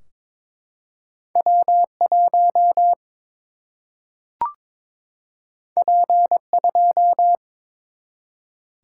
Morse code, heard as a keyed tone, about 22 wpm, 700 Hz. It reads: W 1 P 2
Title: Single-Letter Single-Number 22wpm